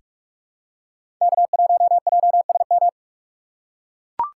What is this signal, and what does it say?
Morse code, heard as a keyed tone, about 41 wpm, 700 Hz. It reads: K1JSM
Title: USA Call Signs - 3x Character Spacing - 45wpm